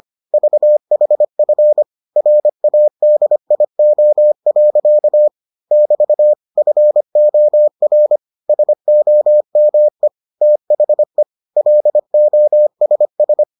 Morse code, heard as a keyed tone, about 25 wpm, 600 Hz. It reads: VHF RADIO. <BT> FOR SOME THE LOSS